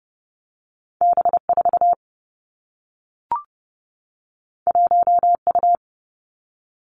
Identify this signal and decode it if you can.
Morse code, heard as a keyed tone, about 30 wpm, 700 Hz. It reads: B 4 1 U
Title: Two Challenging Letter-Number Combinations 30wpm